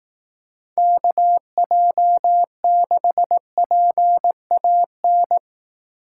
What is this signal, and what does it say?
Morse code, heard as a keyed tone, about 18 wpm, 700 Hz. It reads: KJ6PAN
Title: Speed Racing - USA 2x3 Call Signs 15wpm